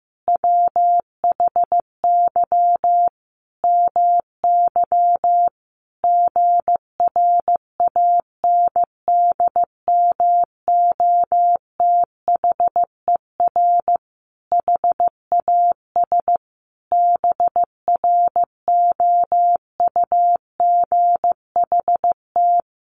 Morse code, heard as a keyed tone, about 15 wpm, 700 Hz. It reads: WHY MY GRANDMOTHER HAS BROUGHT
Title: Sentences from Top 1200 Words 15wpm